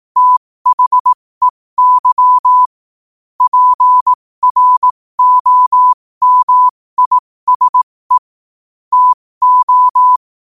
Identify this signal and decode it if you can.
Morse code, heard as a keyed tone, about 18 wpm, spaced as for 15 wpm, 1000 Hz. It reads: THEY PROMISE TO
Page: New quotes every day in morse code at 15 Words per minute.